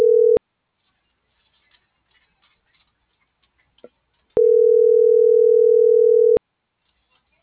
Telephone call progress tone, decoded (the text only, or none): ringback tone